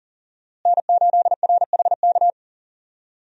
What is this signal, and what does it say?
Morse code, heard as a keyed tone, about 38 wpm, 700 Hz. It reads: N8RHK
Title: USA Call Signs - 5 Characters - Repeated 3x - 40wpm